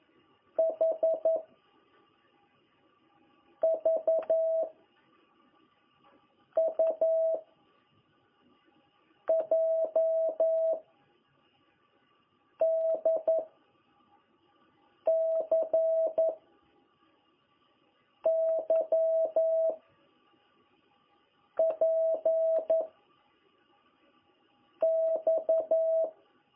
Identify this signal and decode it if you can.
Morse code, heard as a keyed tone, about 11 wpm, 660 Hz. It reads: H V U J D C Y P X
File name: jh8cbh / 2021cw